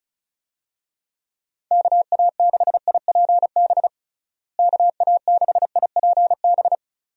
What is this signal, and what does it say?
Morse code, heard as a keyed tone, about 36 wpm, 700 Hz. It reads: KA6IPB KA6IPB
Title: USA Call Signs - 6 Characters - Repeated 2x - 35wpm